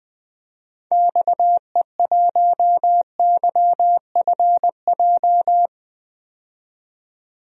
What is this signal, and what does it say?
Morse code, heard as a keyed tone, about 21 wpm, 700 Hz. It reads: XE1YFJ
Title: DXCC-100 Call Signs - 6 Characters - 20wpm